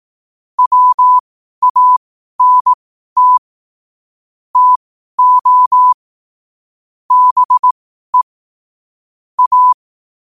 Morse code, heard as a keyed tone, about 18 wpm, spaced as for 12 wpm, 1000 Hz. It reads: WANT TO BE A
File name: qotdfiles / 12wpm